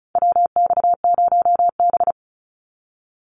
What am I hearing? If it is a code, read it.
Morse code, heard as a keyed tone, about 34 wpm, 700 Hz. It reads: WX0B
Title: Call Signs - 35wpm